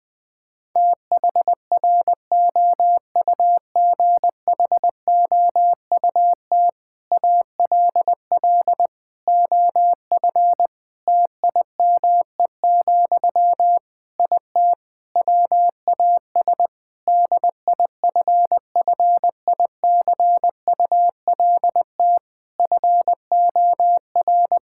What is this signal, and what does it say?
Morse code, heard as a keyed tone, about 20 wpm, 700 Hz. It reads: THROUGHOUT ALL OF TIME, IT WAS DIFFICULT FOR
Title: Sentences from Top 1000 Words - Review Part 5 - 20wpm